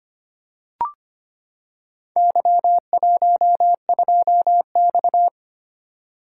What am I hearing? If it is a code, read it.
Morse code, heard as a keyed tone, about 25 wpm, 700 Hz. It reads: Y12X
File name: practice / 25wpm / Four Letter-Number Combinations - 25wpm